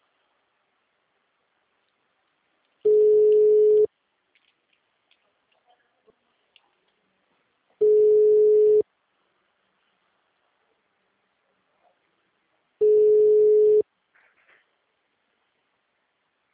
Ringback tone (Europe, 425 Hz).
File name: out- -2007- - - .36982